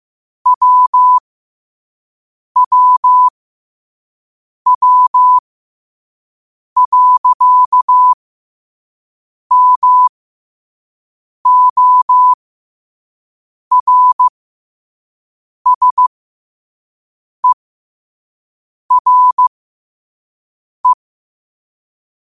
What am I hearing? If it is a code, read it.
Morse code, heard as a keyed tone, about 15 wpm, spaced as for 5 wpm, 1000 Hz. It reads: WWW.MORSERE
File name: 5wpm